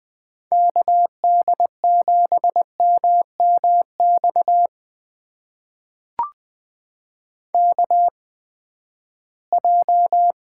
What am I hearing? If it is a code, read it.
Morse code, heard as a keyed tone, about 20 wpm, spaced as for 14 wpm, 700 Hz. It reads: KD7MMX KJ
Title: USA Call Signs - 6 Characters - 8x Character Spacing - 20wpm